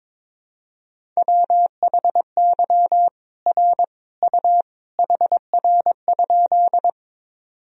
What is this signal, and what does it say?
Morse code, heard as a keyed tone, about 22 wpm, 700 Hz. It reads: WHY R U HR?
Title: Sentences from Top 200 Words and Abbreviations 22wpm